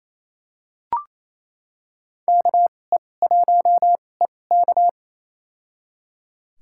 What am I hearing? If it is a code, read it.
Morse code, heard as a keyed tone, about 28 wpm, spaced as for 20 wpm, 700 Hz. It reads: KE1EK
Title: USA Call Signs - 5 Characters - 2x Character Spacing - 28wpm